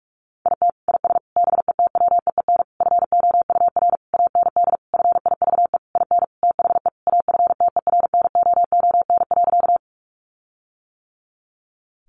Morse code, heard as a keyed tone, about 44 wpm, 700 Hz. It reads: IT IS BETWEEN FOUR AND FIVE IN THE AFTERNOON.
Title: Sentences from Top 1500 Words 45wpm